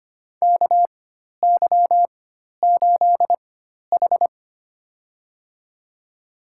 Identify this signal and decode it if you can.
Morse code, heard as a keyed tone, about 25 wpm, spaced as for 12 wpm, 700 Hz. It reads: KY8H
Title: USA Call Signs - 4x Character Spacing - 25wpm